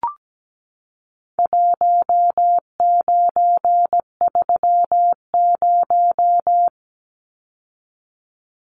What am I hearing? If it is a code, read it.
Morse code, heard as a keyed tone, about 17 wpm, 700 Hz. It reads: 1930
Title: Numbers - Contest Style - 4-Digit 17wpm